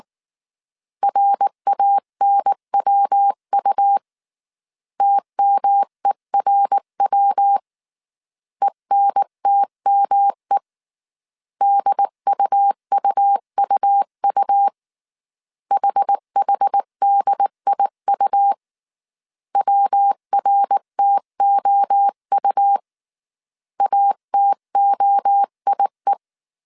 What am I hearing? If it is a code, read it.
Morse code, heard as a keyed tone, about 19 wpm, 800 Hz. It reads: RANWU TMERW ENTME DUUUU HHDIU WRTOU ATOIE